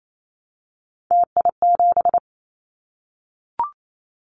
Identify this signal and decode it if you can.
Morse code, heard as a keyed tone, about 28 wpm, 700 Hz. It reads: TI7
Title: Callsign Prefixes for Top 100 DXCC - 28wpm